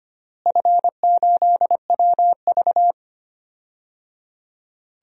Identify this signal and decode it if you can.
Morse code, heard as a keyed tone, about 25 wpm, 700 Hz. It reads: F8WV
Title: DXCC-100 Call Signs - 4 Characters - Repeated 3x - 25wpm